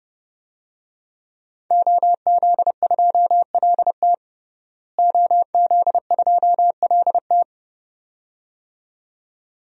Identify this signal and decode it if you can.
Morse code, heard as a keyed tone, about 30 wpm, 700 Hz. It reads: OZ2LT OZ2LT
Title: DXCC-100 Call Signs - 5 Characters - Repeated 2x - 30wpm